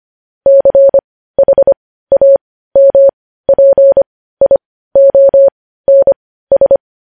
Morse code, heard as a keyed tone, about 25 wpm, spaced as for 15 wpm, 550 Hz. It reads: CHAMPIONS
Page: news-headlines-in-morse-code-at-15-wpm